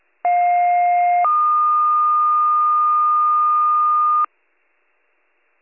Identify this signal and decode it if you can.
Two-tone page, heard as a one-second tone, then a three-second tone, about 710 then 1220 Hz. It is heard from 0.2 to 4.3 s.